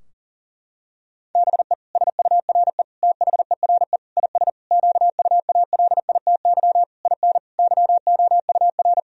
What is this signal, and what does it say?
Morse code, heard as a keyed tone, about 37 wpm, 700 Hz. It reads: BE SURE THERE IS QUALITY IN YOUR
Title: Sentences from Top 1000 Words - Review Part 4 - 40wpm